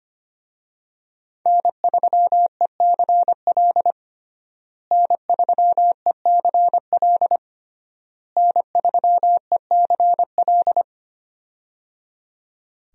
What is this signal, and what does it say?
Morse code, heard as a keyed tone, about 25 wpm, 700 Hz. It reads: N3ECL N3ECL N3ECL
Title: USA Call Signs - 5 Characters - Repeated 3x - 25wpm